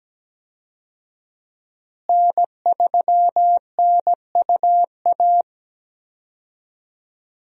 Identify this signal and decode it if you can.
Morse code, heard as a keyed tone, about 17 wpm, 700 Hz. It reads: N3NUA
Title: USA Call Signs - 5 Characters - Repeated 3x - 17wpm